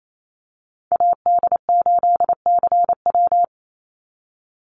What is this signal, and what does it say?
Morse code, heard as a keyed tone, about 28 wpm, 700 Hz. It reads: AD8CW
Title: Call Signs - 28wpm